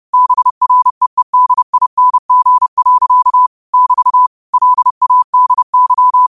Morse code, heard as a keyed tone, about 30 wpm, 1000 Hz. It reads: BREEDING. <BT> LADY
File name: qotdfiles / 30wpm